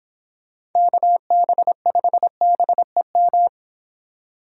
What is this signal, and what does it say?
Morse code, heard as a keyed tone, about 26 wpm, 700 Hz. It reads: KB5BEM